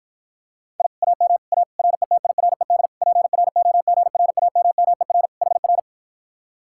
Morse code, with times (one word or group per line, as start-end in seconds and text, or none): 0.8-0.9 I
1.0-1.4 AM
1.5-1.6 A
1.8-2.9 RETIRED
3.0-5.3 PROGRAMMER
5.4-5.8 HR